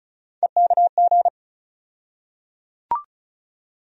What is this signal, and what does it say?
Morse code, heard as a keyed tone, about 34 wpm, 700 Hz. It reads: EKG
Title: USA Call Sign Postfixes - 3 Letters 35wpm